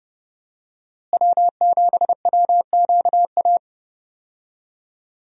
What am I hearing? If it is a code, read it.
Morse code, heard as a keyed tone, about 30 wpm, 700 Hz. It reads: W7WQA
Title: USA Call Signs - 5 Characters - Repeated 3x - 30wpm